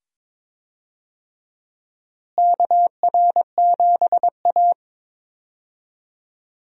Morse code, heard as a keyed tone, about 22 wpm, 700 Hz. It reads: KR7A
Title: USA Call Signs - 4 Characters - Repeated 2x - 22wpm